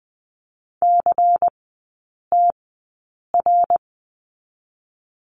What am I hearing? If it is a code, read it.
Morse code, heard as a keyed tone, about 20 wpm, spaced as for 8 wpm, 690 Hz. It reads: CTR
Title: 093 Lesson 16e - Letter c - Three Characters 20wpm